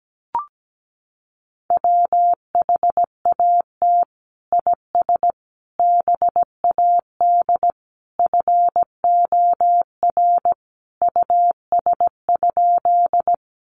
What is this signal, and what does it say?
Morse code, heard as a keyed tone, about 17 wpm, 700 Hz. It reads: WHAT IS BAD FOR US?
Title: Sentences from Top 700 Words 17wpm